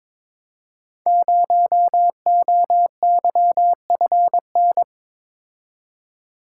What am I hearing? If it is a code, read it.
Morse code, heard as a keyed tone, about 23 wpm, 700 Hz. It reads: 0OYFN
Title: Code Groups - 5 Letter:Numbers - 22wpm